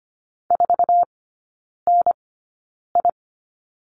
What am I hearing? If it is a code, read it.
Morse code, heard as a keyed tone, about 25 wpm, spaced as for 9 wpm, 690 Hz. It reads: 4NI
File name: Lesson 14e - Number 2 - Three Characters 25wpm